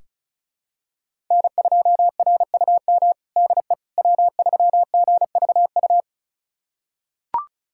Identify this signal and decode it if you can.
Morse code, heard as a keyed tone, about 34 wpm, 700 Hz. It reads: N2RUM DE W3GVU
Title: USA Callsign DE Callsign - 5 Characters - 35wpm